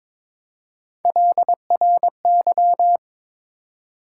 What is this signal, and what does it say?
Morse code, heard as a keyed tone, about 22 wpm, 700 Hz. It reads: LRY
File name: practice / 22wpm / Three Letter-Number Combinations - 22wpm